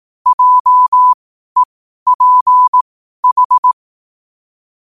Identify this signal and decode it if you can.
Morse code, heard as a keyed tone, about 18 wpm, spaced as for 12 wpm, 1000 Hz. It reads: JEPH
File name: qotdfiles / 12wpm